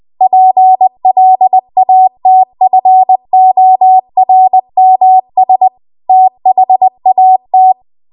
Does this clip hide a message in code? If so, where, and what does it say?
Yes: Morse code from 0.2 to 7.7 s, heard as a keyed tone, about 20 wpm, 750 Hz. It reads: PLATFORMS THAT